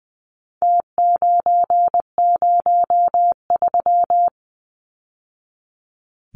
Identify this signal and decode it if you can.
Morse code, heard as a keyed tone, about 20 wpm, 700 Hz. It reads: T903
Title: Numbers - Contest Style - 4-Digit 20wpm